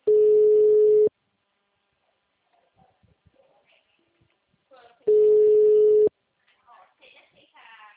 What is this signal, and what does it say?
Ringback tone (Europe, 425 Hz), with speech.
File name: out- -2064- - - .94450